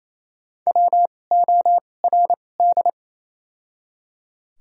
Morse code, heard as a keyed tone, about 28 wpm, 700 Hz. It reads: W O R D
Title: 2x Character Spacing - Top 200 Words - 28wpm